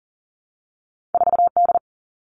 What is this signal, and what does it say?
Morse code, heard as a keyed tone, about 41 wpm, 700 Hz. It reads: 4D
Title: All Cut Numbers - 2-Digit 40wpm